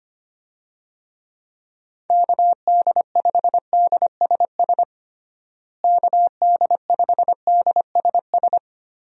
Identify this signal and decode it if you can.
Morse code, heard as a keyed tone, about 25 wpm, 700 Hz. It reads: KD5DSS KD5DSS
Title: USA Call Signs - 6 Characters - Repeated 2x - 25wpm